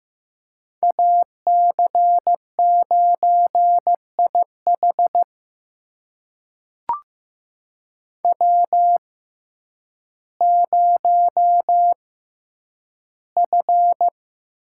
Morse code, heard as a keyed tone, about 15 wpm, spaced as for 9 wpm, 700 Hz. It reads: AC9IH W0F
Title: USA Call Signs - 5 Characters - 6x Character Spacing - 15wpm